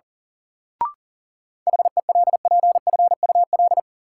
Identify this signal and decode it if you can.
Morse code, heard as a keyed tone, about 37 wpm, 700 Hz. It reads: HELPFUL
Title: Sentences from Top 1000 Words - Review Part 6 - New Word Drill - 40wpm